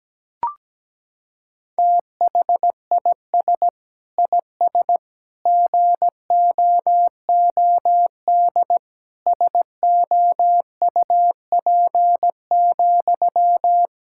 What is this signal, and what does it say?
Morse code, heard as a keyed tone, about 17 wpm, 700 Hz. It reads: THIS IS GOOD SOUP,
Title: Sentences from Top 1000 Words - Review Part 6 - 17wpm